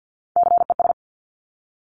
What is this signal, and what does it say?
Morse code, heard as a keyed tone, about 50 wpm, 700 Hz. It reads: CES
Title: Top Three Letter Combinations in Words 50wpm